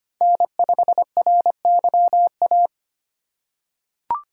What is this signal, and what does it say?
Morse code, heard as a keyed tone, about 25 wpm, 700 Hz. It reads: N5RYA
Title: USA Call Signs - 5 Characters - 4x Character Spacing - 25wpm